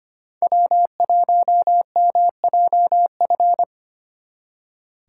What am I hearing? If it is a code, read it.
Morse code, heard as a keyed tone, about 29 wpm, 700 Hz. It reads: W1MJF
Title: Speed Racing - USA 1x3 Call Signs 25wpm